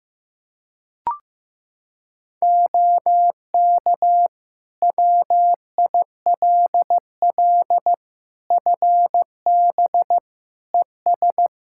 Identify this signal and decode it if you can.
Morse code, heard as a keyed tone, about 16 wpm, 700 Hz. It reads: OK WILL FB ES
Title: Practice QSO - Exchange 2 - Info - 15wpm